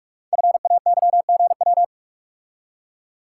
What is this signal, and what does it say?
Morse code, heard as a keyed tone, about 44 wpm, 690 Hz. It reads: FAYGW
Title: Code Groups - 5 Letters - 45wpm